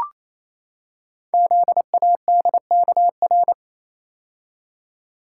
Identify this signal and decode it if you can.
Morse code, heard as a keyed tone, about 28 wpm, 700 Hz. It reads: ZADKR
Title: Code Groups - 5 Letters - 28wpm